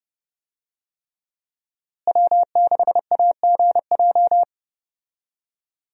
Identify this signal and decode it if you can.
Morse code, heard as a keyed tone, about 31 wpm, 700 Hz. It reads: W6AGJ
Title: USA Call Signs - 5 Characters - Repeated 3x - 30wpm